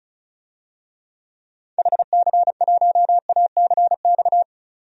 Morse code, heard as a keyed tone, about 34 wpm, 700 Hz. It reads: HC1ACX
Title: DXCC-100 Call Signs - 6 Characters - 35wpm